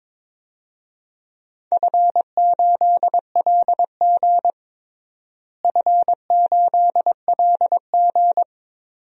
Morse code, heard as a keyed tone, about 22 wpm, 700 Hz. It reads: F8LG F8LG
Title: DXCC-100 Call Signs - 4 Characters - Repeated 2x - 22wpm